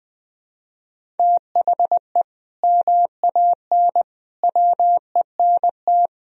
Morse code, heard as a keyed tone, about 21 wpm, 700 Hz. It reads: THE MAN WENT
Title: Sentences from Top 1000 Words - Review Part 1 - 20wpm